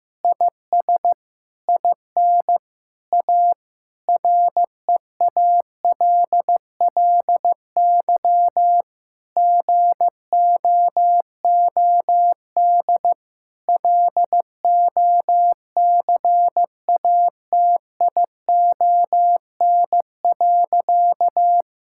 Morse code, heard as a keyed tone, about 15 wpm, 700 Hz. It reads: IS IN A REALLY GOOD LOCATION.
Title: Sentences from Top 1000 Words - Review Part 6 - 15wpm